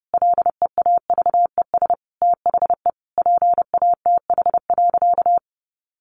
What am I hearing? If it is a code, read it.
Morse code, heard as a keyed tone, about 30 wpm, 700 Hz. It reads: LEAVES THE PATH.
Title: A Sound Of Thunder 30wpm